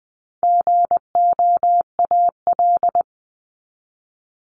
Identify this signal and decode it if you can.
Morse code, heard as a keyed tone, about 20 wpm, 700 Hz. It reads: GOAL
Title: Top 751-1000 Words 20wpm